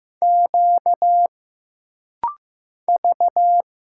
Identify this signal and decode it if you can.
Morse code, heard as a keyed tone, about 15 wpm, 700 Hz. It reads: Q V
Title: Single Letters - Rapid-Fire 15wpm